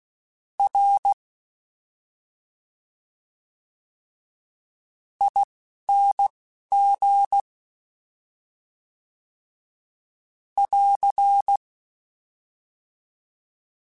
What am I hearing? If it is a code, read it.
Morse code, heard as a keyed tone, about 16 wpm, spaced as for 12 wpm, 800 Hz. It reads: R ING <AR>
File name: Special INGs-mp3s / Special INGs-mp3s / Slow ing @10